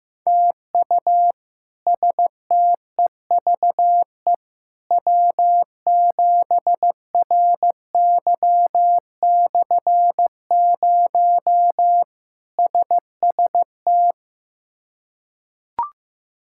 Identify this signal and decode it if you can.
Morse code, heard as a keyed tone, about 16 wpm, 700 Hz. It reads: TU STEVE W7RY/0 SST
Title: K1USN SST QSO Practice - 15wpm